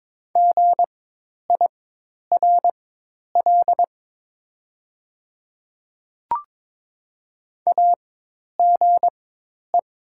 Morse code, heard as a keyed tone, about 22 wpm, spaced as for 10 wpm, 700 Hz. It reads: GIRL AGE
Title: 4x Character Spacing - Top 500 Words - 22wpm